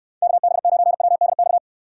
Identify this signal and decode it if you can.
Morse code, heard as a keyed tone, about 66 wpm, 700 Hz. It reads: KB9WDX